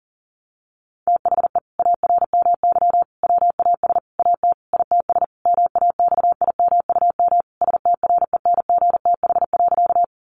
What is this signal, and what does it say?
Morse code, heard as a keyed tone, about 40 wpm, 700 Hz. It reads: THE ARMY WAS AT ITS MAXIMUM STRENGTH.
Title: Sentences from Top 1300 Words 40wpm